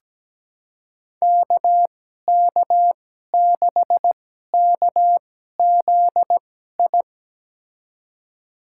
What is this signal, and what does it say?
Morse code, heard as a keyed tone, about 17 wpm, spaced as for 12 wpm, 700 Hz. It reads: KK6KZI
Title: USA Call Signs - 6 Characters - 2x Character Spacing - 17wpm